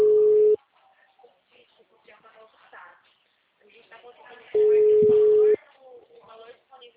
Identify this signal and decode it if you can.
Ringback tone (Europe, 425 Hz), with speech.